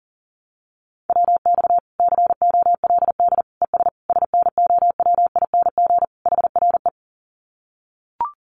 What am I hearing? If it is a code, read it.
Morse code, heard as a keyed tone, about 40 wpm, 700 Hz. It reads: WX COLD ES SNOWING HRE
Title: Common QSO Phrases 40wpm